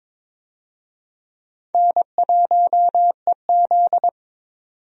Morse code, heard as a keyed tone, about 23 wpm, 700 Hz. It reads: N1EZ
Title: USA Call Signs - 4 Characters - 22wpm